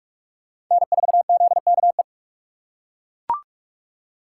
Morse code, heard as a keyed tone, about 41 wpm, 700 Hz. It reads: N4ZKE
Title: USA Call Signs - 5 Characters - 3x Character Spacing - 45wpm